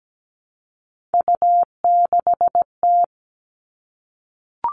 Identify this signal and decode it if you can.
Morse code, heard as a keyed tone, about 17 wpm, 700 Hz. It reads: U6T
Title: All Cut Numbers - 3-Digit 17wpm